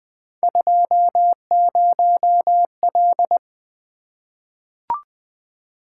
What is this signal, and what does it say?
Morse code, heard as a keyed tone, about 20 wpm, 700 Hz. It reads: 20L
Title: Three Letter-Number Combinations - 20wpm